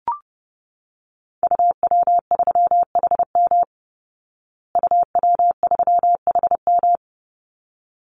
Morse code, heard as a keyed tone, about 30 wpm, 700 Hz. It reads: UW3HM UW3HM
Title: Call Signs Repeated 3x - 30wpm